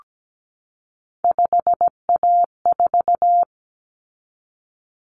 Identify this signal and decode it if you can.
Morse code, heard as a keyed tone, about 17 wpm, 700 Hz. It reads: 5A4
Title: Common Cut Numbers - 3-Digit 17wpm